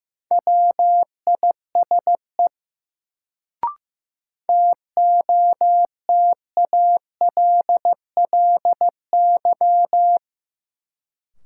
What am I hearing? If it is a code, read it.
Morse code, heard as a keyed tone, about 15 wpm, 700 Hz. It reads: WISE TOTALLY
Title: Sentences from Top 1600 Words - New Word Drill Alt - 15wpm